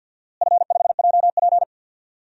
Morse code, heard as a keyed tone, about 47 wpm, 700 Hz. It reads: F5JP
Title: DXCC-100 Call Signs - 4 Characters - Repeated 2x - 50wpm